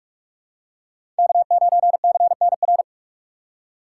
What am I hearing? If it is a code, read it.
Morse code, heard as a keyed tone, about 44 wpm, 700 Hz. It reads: K9CNR
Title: USA Call Signs - 5 Characters - 6x Character Spacing - 45wpm